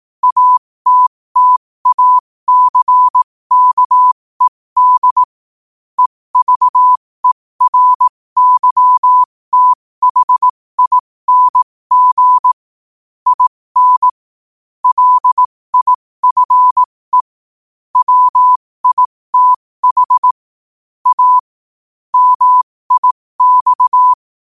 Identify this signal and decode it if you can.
Morse code, heard as a keyed tone, about 18 wpm, spaced as for 15 wpm, 1000 Hz. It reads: ATTACKED EVERYTHING IN LIFE WITH A MIX
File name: qotdfiles / 15wpm